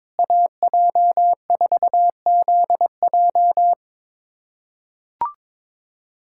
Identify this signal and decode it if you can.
Morse code, heard as a keyed tone, about 21 wpm, 700 Hz. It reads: AJ4ZJ
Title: USA Callsign DE Callsign - 5 Characters - 22wpm